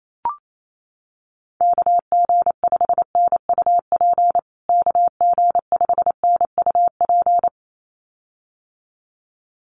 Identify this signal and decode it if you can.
Morse code, heard as a keyed tone, about 28 wpm, 700 Hz. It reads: KG5NUP KG5NUP
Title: USA Call Signs Repeated 2x 28wpm